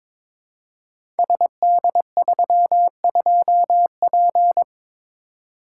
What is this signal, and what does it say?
Morse code, heard as a keyed tone, about 22 wpm, 700 Hz. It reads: SD32P
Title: Code Groups - 5 Letter:Numbers - 22wpm